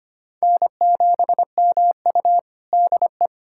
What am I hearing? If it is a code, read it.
Morse code, heard as a keyed tone, about 25 wpm, 700 Hz. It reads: N7MU DE
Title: USA Callsign DE Callsign - 4 Characters - 25wpm